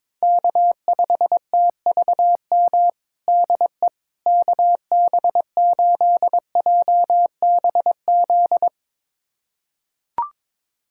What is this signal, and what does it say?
Morse code, heard as a keyed tone, about 22 wpm, 700 Hz. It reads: K5TVM DE KB8JBZ
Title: USA Callsign DE Callsign 22wpm